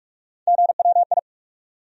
Morse code, heard as a keyed tone, about 46 wpm, 700 Hz. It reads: GWI
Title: USA Call Sign Postfixes - 3 Letters 45wpm